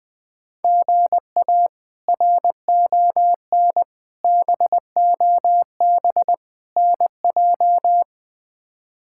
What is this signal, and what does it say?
Morse code, heard as a keyed tone, about 21 wpm, 700 Hz. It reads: GA RON BOB NJ
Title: K1USN SST QSO Practice - 20wpm